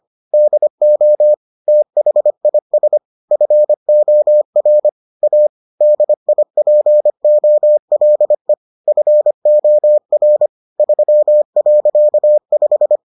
Morse code, heard as a keyed tone, about 25 wpm, 600 Hz. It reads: DO THIS FOR A DIPOLE FOR 3.5